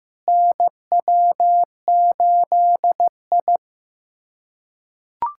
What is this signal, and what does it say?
Morse code, heard as a keyed tone, about 15 wpm, 700 Hz. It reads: NW8I
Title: USA Callsign DE Callsign - 4 Characters - 15wpm